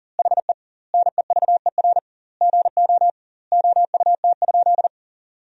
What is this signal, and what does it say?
Morse code, heard as a keyed tone, about 40 wpm, 700 Hz. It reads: HE NEVER GO OUT?
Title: 2x Word-Spacing - Sentences from Top 300 Words - 40wpm